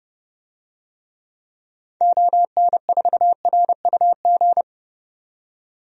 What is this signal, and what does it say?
Morse code, heard as a keyed tone, about 30 wpm, 700 Hz. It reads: ON4RUG
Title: DXCC-100 Call Signs - 6 Characters - 30wpm